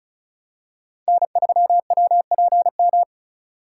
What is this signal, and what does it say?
Morse code, heard as a keyed tone, about 36 wpm, 700 Hz. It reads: N3WPM
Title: USA Call Signs - 3x Character Spacing - 35wpm